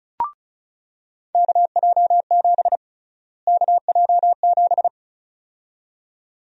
Morse code, heard as a keyed tone, about 36 wpm, 700 Hz. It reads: KJ7 KJ7
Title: USA Call Sign Prefixes - Repeated 2x - 35wpm